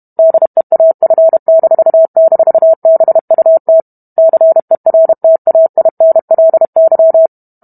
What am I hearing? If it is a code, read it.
Morse code, heard as a keyed tone, about 32 wpm, 650 Hz. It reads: DEAF--BUT CERTAINLY